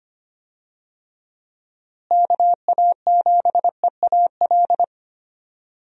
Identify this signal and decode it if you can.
Morse code, heard as a keyed tone, about 25 wpm, 700 Hz. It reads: KA7EAL
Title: USA Call Signs - 6 Characters - Repeated 3x - 25wpm